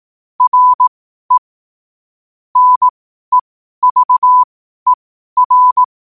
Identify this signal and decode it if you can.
Morse code, heard as a keyed tone, about 18 wpm, spaced as for 12 wpm, 1000 Hz. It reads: RE NEVER
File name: qotdfiles / 12wpm